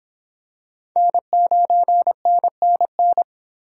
Morse code, heard as a keyed tone, about 25 wpm, 700 Hz. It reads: N9NNN